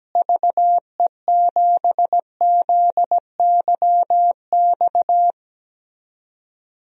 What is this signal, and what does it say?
Morse code, heard as a keyed tone, about 17 wpm, 700 Hz. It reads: VE7ZYX
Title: DXCC-100 Call Signs - 6 Characters - Repeated 2x - 17wpm